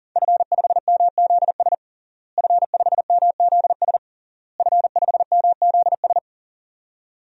Morse code, heard as a keyed tone, about 40 wpm, 700 Hz. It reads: F5MZS F5MZS F5MZS
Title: DXCC-100 Call Signs - 5 Characters - Repeated 3x - 40wpm